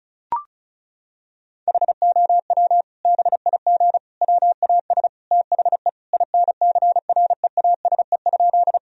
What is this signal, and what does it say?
Morse code, heard as a keyed tone, about 36 wpm, 700 Hz. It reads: HOW BIG WAS THE INCREASE?
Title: Sentences from Top 1000 Words - Review Part 4 - 35wpm